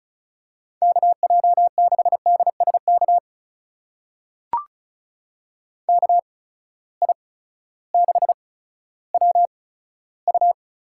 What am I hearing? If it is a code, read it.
Morse code, heard as a keyed tone, about 36 wpm, spaced as for 16 wpm, 700 Hz. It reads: KJ6DSK KI6WU
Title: USA Call Signs - 6 Characters - 8x Character Spacing - 35wpm